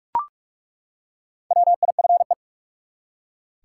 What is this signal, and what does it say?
Morse code, heard as a keyed tone, about 42 wpm, 700 Hz. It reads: WIFE
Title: Four Letter Words - 45wpm